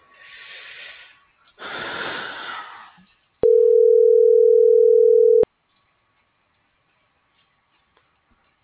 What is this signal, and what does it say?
Ringback tone (North America), 440 + 480 Hz.